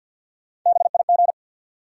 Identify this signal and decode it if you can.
Morse code, heard as a keyed tone, about 53 wpm, 700 Hz. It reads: BIG
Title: Sentences from Top 200 Words - New Word Drill - 50wpm